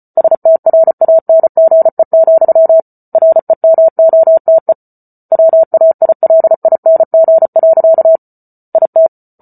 Morse code, heard as a keyed tone, about 34 wpm, 650 Hz. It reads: STRANGE, REMOTE WAILING. IT